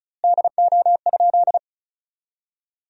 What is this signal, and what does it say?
Morse code, heard as a keyed tone, about 35 wpm, 700 Hz. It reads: DO?
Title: 2x Word-Spacing - Sentences from Top 100 Words - 35wpm